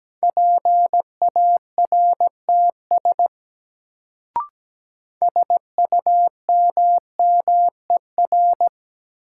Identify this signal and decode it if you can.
Morse code, heard as a keyed tone, about 17 wpm, 700 Hz. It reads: PARTS SUMMER
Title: Sentences from Top 1000 Words - Review Part 6 - New Word Drill - 17wpm